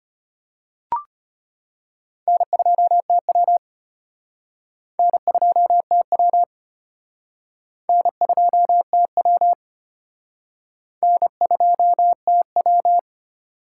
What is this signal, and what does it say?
Morse code, heard as a keyed tone, about 34 wpm, 700 Hz. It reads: N2TW N2TW N2TW N2TW